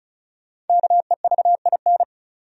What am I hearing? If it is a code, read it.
Morse code, heard as a keyed tone, about 36 wpm, 700 Hz. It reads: KEVIN
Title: USA Common Names - Max 5 Letters - 35wpm